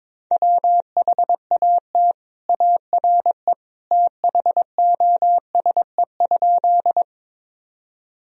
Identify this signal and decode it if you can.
Morse code, heard as a keyed tone, about 21 wpm, 700 Hz. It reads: WHAT ARE THOSE?
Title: Sentences from Top 1000 Words - Review Part 1 - 22wpm